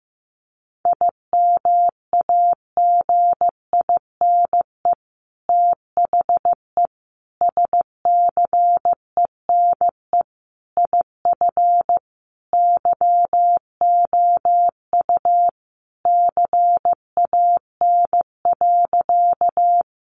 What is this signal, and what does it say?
Morse code, heard as a keyed tone, about 15 wpm, 700 Hz. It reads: IMAGINE THE SCENE IF YOU CAN.
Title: Sentences from Top 1400 Words 15wpm